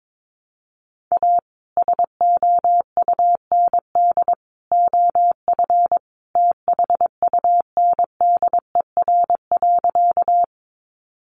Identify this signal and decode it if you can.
Morse code, heard as a keyed tone, about 22 wpm, 700 Hz. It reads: A SOUND OF THUNDER.
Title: A Sound Of Thunder 22wpm